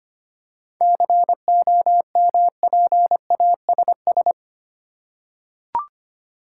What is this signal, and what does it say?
Morse code, heard as a keyed tone, about 25 wpm, 690 Hz. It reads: COMPASS
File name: practice / 25wpm / Easy Words with Hints - 25wpm